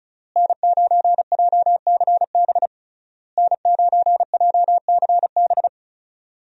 Morse code, heard as a keyed tone, about 36 wpm, 700 Hz. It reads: N9JCB N9JCB
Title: USA Call Signs - 5 Characters - Repeated 3x - 35wpm